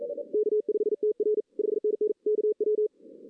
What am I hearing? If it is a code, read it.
Morse code, heard as a keyed tone, about 46 wpm, 410 Hz. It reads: K5TR 5NN KW
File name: 2016 arrl dx cw / 2016-ARRL-DX-CW-K5TR- - 0-3